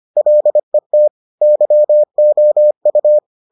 Morse code, heard as a keyed tone, about 25 wpm, 600 Hz. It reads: LET YOU